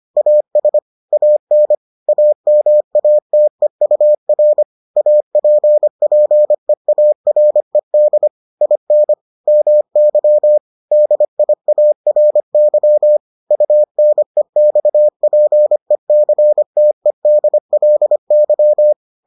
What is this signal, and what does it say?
Morse code, heard as a keyed tone, about 25 wpm, 600 Hz. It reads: AS AN AMATEUR APPEARED IN MY DIARY UNEXPECTEDLY